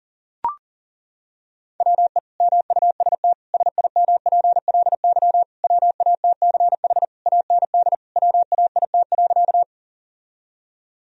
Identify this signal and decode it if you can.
Morse code, heard as a keyed tone, about 37 wpm, 700 Hz. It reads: WE MUST SIMPLY WATCH AND WAIT.
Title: Sentences from Top 1000 Words - Review Part 4 - 40wpm